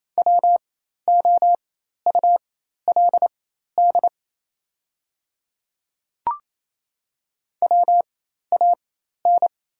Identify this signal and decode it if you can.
Morse code, heard as a keyed tone, about 27 wpm, spaced as for 13 wpm, 700 Hz. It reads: WOULD WAN
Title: 4x Character Spacing - Top 200 Words - 28wpm